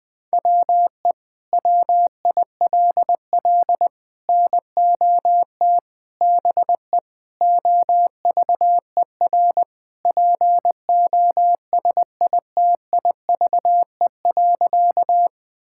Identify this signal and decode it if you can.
Morse code, heard as a keyed tone, about 21 wpm, 700 Hz. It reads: WE WILL NOT BE OVER POSITIVE.
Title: Sentences from Top 1000 Words - Review Part 5 - 20wpm